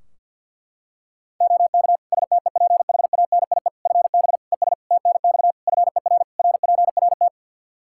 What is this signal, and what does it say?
Morse code, heard as a keyed tone, about 50 wpm, 700 Hz. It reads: OK STEPHANIE FB ES TNX FER RPRT
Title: Practice QSO - Exchange 2 - Info - 50wpm